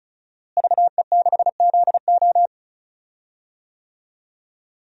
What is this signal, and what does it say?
Morse code, heard as a keyed tone, about 36 wpm, 700 Hz. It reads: VE6ZO
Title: DXCC-100 Call Signs - 5 Characters - Repeated 2x - 35wpm